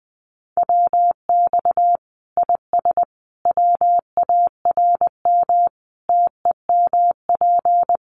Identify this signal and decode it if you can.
Morse code, heard as a keyed tone, about 19 wpm, 700 Hz. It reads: WX IS WARM TEMP
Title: Common QSO Phrases 20wpm